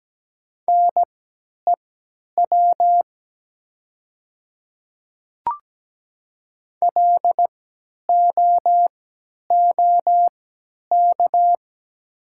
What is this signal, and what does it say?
Morse code, heard as a keyed tone, about 17 wpm, spaced as for 10 wpm, 700 Hz. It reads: NEW LOOK
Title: 3x Character Spacing - Top 300 Words - 17wpm